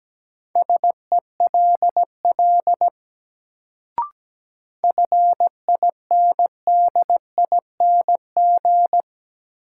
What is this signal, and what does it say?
Morse code, heard as a keyed tone, about 17 wpm, 700 Hz. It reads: SELL FINDING
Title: Sentences from Top 1100 Words - New Word Drill Alt - 17wpm